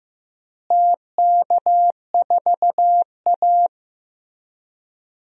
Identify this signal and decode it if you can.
Morse code, heard as a keyed tone, about 15 wpm, 700 Hz. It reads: TK4A
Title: DXCC-100 Call Signs - 4 Characters - Repeated 2x - 15wpm